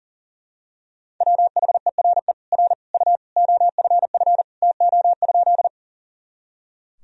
Morse code, heard as a keyed tone, about 37 wpm, 700 Hz. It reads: WHERE R U OFF TO?
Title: Sentences from Top 200 Words and Abbreviations 40wpm